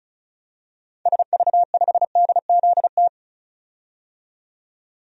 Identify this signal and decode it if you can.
Morse code, heard as a keyed tone, about 34 wpm, 700 Hz. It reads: SV5DZT
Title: DXCC-100 Call Signs - 6 Characters - 35wpm